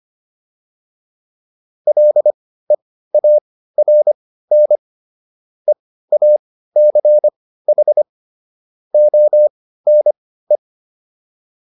Morse code, heard as a keyed tone, about 25 wpm, spaced as for 15 wpm, 600 Hz. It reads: LEARN EACH ONE